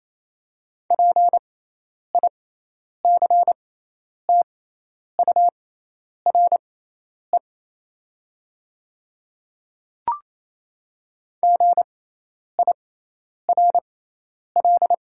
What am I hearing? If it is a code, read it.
Morse code, heard as a keyed tone, about 28 wpm, spaced as for 10 wpm, 700 Hz. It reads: PICTURE GIRL
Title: 6x Character Spacing - Top 500 Words - 28wpm